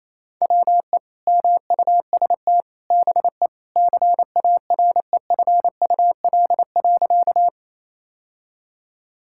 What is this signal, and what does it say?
Morse code, heard as a keyed tone, about 28 wpm, 700 Hz. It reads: WE MUST BE CAREFUL.
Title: Sentences from Top 1600 Words 28wpm